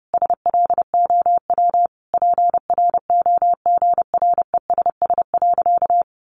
Morse code, heard as a keyed tone, about 30 wpm, 700 Hz. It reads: SLOW PROGRESS.
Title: Sentences from Top 1400 Words 30wpm